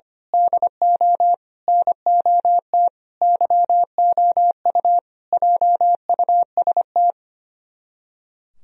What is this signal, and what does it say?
Morse code, heard as a keyed tone, about 24 wpm, 700 Hz. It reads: DO NOT YOU JUST
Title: Speed-Racing - Sets of 5 Words 25wpm